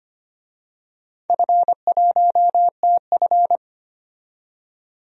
Morse code, heard as a keyed tone, about 25 wpm, 700 Hz. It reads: F1TF
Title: DXCC-100 Call Signs - 4 Characters - 25wpm